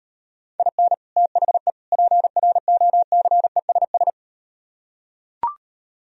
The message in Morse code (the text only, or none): IN THE PROCESS